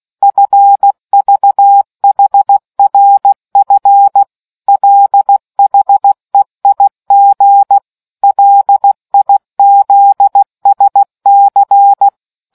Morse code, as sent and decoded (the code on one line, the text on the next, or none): ..-. ...- .... .-. ..-. / .-.. .... .. --. / .-.. .. --.. ... -.-.
FVHRF LHEIG LIZSC